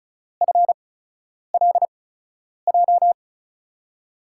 Morse code, heard as a keyed tone, about 36 wpm, spaced as for 10 wpm, 700 Hz. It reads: FLJ
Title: USA Call Signs - 6 Characters - 8x Character Spacing - 35wpm